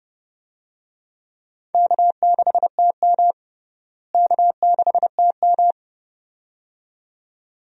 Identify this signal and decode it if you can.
Morse code, heard as a keyed tone, about 30 wpm, 700 Hz. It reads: K6TM K6TM
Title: USA Call Signs - 4 Characters - Repeated 2x - 30wpm